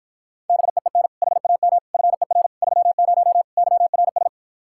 Morse code, heard as a keyed tone, about 43 wpm, 700 Hz. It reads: BEEN HAM FER 30 YRS
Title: Speed-Racing Common QSO Phrases - 35wpm